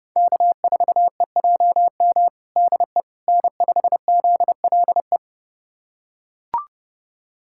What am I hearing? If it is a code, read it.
Morse code, heard as a keyed tone, about 30 wpm, 700 Hz. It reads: K4EJM DE N5ZLE
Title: USA Callsign DE Callsign - 5 Characters - 30wpm